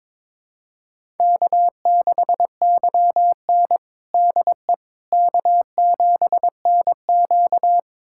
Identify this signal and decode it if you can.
Morse code, heard as a keyed tone, about 22 wpm, 700 Hz. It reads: K6YN DE K7NQ
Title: USA Callsign DE Callsign - 4 Characters - 22wpm